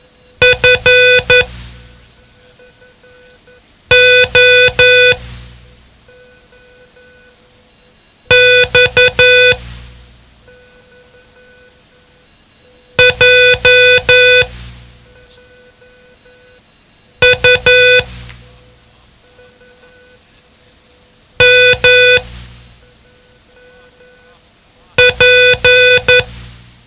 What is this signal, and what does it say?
Morse code, heard as a keyed tone, about 11 wpm, spaced as for 3 wpm, 500 Hz. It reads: FOX JUMP
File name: ham radio / CodeQuick / Code Quick 07